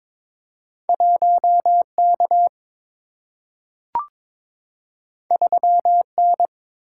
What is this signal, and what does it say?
Morse code, heard as a keyed tone, about 22 wpm, 690 Hz. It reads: 1K 3N
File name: practice / 22wpm / Two Letter-Number Combinations - 22wpm